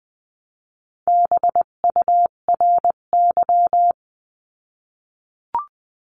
Morse code, heard as a keyed tone, about 20 wpm, 700 Hz. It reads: BURY
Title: 258 Lesson 42c - Words 16-20wpm